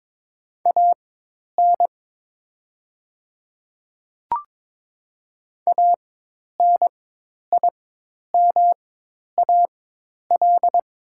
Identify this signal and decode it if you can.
Morse code, heard as a keyed tone, about 23 wpm, spaced as for 10 wpm, 690 Hz. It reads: AN ANIMAL
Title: 4x Character Spacing - Top 200 Words - 22wpm